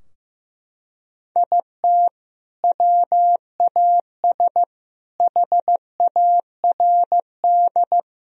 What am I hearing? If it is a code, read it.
Morse code, heard as a keyed tone, about 15 wpm, 700 Hz. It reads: IT WAS HARD
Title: Sentences from Top 1000 Words - Review Part 4 - 15wpm